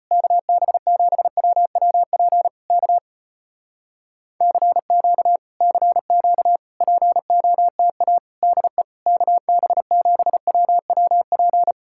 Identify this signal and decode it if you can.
Morse code, heard as a keyed tone, about 35 wpm, 700 Hz. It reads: KB7WWP K CQ CQ POTA DE KB7WWP